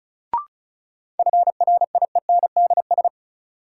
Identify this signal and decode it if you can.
Morse code, heard as a keyed tone, about 36 wpm, 700 Hz. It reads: FRIENDS
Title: Sentences from Top 1000 Words - Review Part 5 - New Word Drill - 35wpm